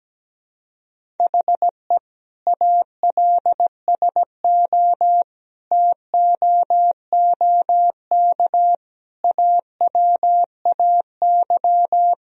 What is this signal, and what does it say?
Morse code, heard as a keyed tone, about 17 wpm, 700 Hz. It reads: HE ALSO TOOK AWAY
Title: Sentences from Top 1000 Words - Review Part 1 - 17wpm